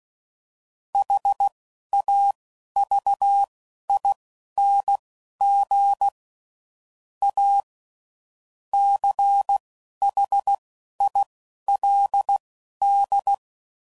Morse code, heard as a keyed tone, about 16 wpm, spaced as for 11 wpm, 800 Hz. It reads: HAVING A CHILD